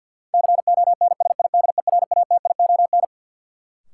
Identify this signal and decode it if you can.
Morse code, heard as a keyed tone, about 53 wpm, 700 Hz. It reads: CONSIDERATION
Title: Sentences from Top 1500 Words - New Word Drill - 50wpm